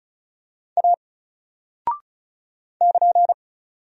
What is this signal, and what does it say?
Morse code, heard as a keyed tone, about 35 wpm, 700 Hz. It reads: A <KN>
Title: Single Character (Letter:Number:Punctuation:Prosign) 35wpm